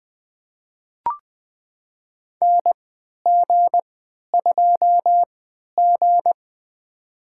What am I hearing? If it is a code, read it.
Morse code, heard as a keyed tone, about 21 wpm, spaced as for 11 wpm, 700 Hz. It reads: NG2G
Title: USA Call Signs - 4 Characters - 3x Character Spacing - 20wpm